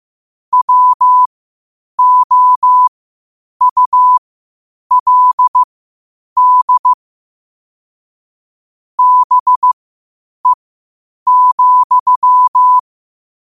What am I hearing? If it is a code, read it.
Morse code, heard as a keyed tone, about 15 wpm, spaced as for 8 wpm, 1000 Hz. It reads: WOULD BE,
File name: qotdfiles / 8wpm